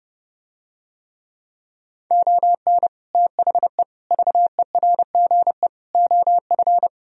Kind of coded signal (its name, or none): Morse code